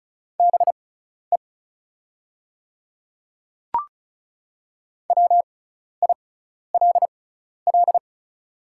Morse code, heard as a keyed tone, about 36 wpm, spaced as for 12 wpm, 700 Hz. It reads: BE WILL
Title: 6x Character Spacing - Top 100 Words - 35wpm